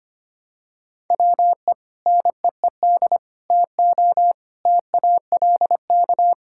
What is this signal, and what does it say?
Morse code, heard as a keyed tone, about 25 wpm, 700 Hz. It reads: WE NEED TO TALK
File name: practice / 25wpm / Sentences with 2-Word Phrases - 25wpm